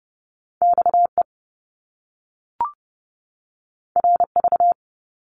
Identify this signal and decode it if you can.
Morse code, heard as a keyed tone, about 30 wpm, 700 Hz. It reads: XE RV
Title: Top Two Letter Combinations in Words 30wpm